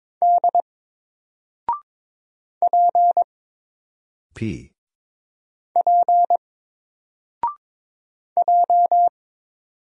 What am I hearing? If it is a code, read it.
Morse code, heard as a keyed tone, about 22 wpm, 700 Hz. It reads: D P P J
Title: Single Letter-Number 22wpm